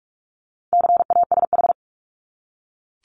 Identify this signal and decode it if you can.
Morse code, heard as a keyed tone, about 44 wpm, 700 Hz. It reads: CASH
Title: Top 501-750 Words 45wpm